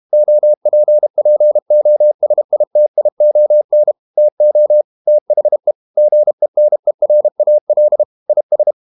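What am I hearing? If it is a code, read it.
Morse code, heard as a keyed tone, about 32 wpm, 600 Hz. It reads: OPPOSITION TO THE GENERAL IS
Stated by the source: Morse code practice at 32 wpm from CW med Gnister, reading Ken Follett's novel Never in English.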